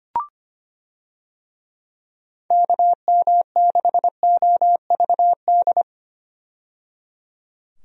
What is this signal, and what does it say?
Morse code, heard as a keyed tone, about 25 wpm, 700 Hz. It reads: KM6OVD
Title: USA Call Signs - 6 Characters - 25wpm